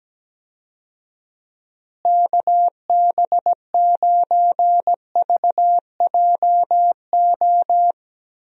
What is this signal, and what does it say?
Morse code, heard as a keyed tone, about 17 wpm, 700 Hz. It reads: KB9VJO
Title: USA Call Signs - 6 Characters - Repeated 3x - 17wpm